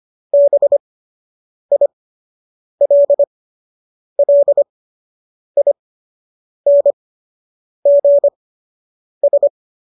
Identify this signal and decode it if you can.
Morse code, heard as a keyed tone, about 25 wpm, spaced as for 8 wpm, 570 Hz. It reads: BILLINGS